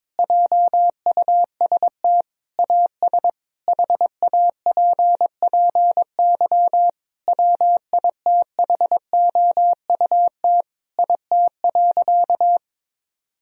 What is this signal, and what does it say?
Morse code, heard as a keyed tone, about 22 wpm, 700 Hz. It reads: JUST AS HAPPY WITHOUT IT.
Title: Sentences from Top 1000 Words - Review Part 3 - 22wpm